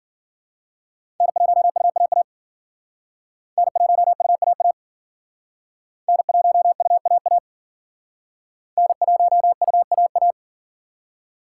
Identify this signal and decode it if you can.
Morse code, heard as a keyed tone, about 52 wpm, 700 Hz. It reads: N1UAA N1UAA N1UAA N1UAA